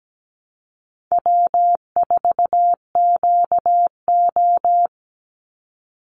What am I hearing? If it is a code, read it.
Morse code, heard as a keyed tone, about 17 wpm, 700 Hz. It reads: W4QO
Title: Call Signs - 17wpm